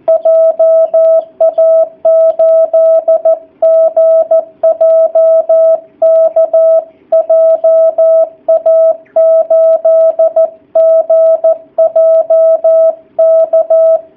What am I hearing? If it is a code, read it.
Morse code, heard as a keyed tone, about 14 wpm, 660 Hz. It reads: JA8GJK JA8GJK
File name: jh8cbh / 2019cw / sp13- w